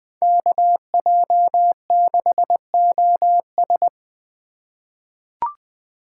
Morse code, heard as a keyed tone, about 20 wpm, 700 Hz. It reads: KJ6OS
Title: USA Call Signs - 5 Characters - 2x Character Spacing - 20wpm